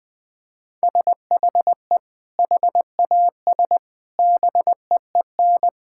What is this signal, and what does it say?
Morse code, heard as a keyed tone, about 20 wpm, 700 Hz. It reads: SHE HAS BEEN
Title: Sentences from Top 1000 Words - Review Part 6 - 20wpm